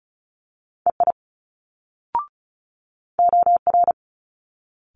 Morse code, heard as a keyed tone, about 36 wpm, 700 Hz. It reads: EI OR
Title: Multi-letter Phonemes Quick Fire 35wpm